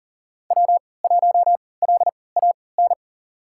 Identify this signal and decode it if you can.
Morse code, heard as a keyed tone, about 40 wpm, spaced as for 23 wpm, 700 Hz. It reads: W1LAN
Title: USA Call Signs - 5 Characters - 3x Character Spacing - 40wpm